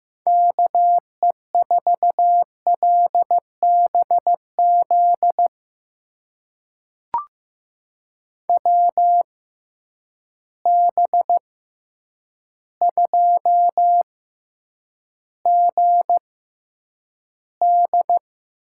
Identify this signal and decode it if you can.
Morse code, heard as a keyed tone, about 15 wpm, spaced as for 8 wpm, 700 Hz. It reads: KE4LBZ WB2GD
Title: USA Call Signs - 6 Characters - 6x Character Spacing - 15wpm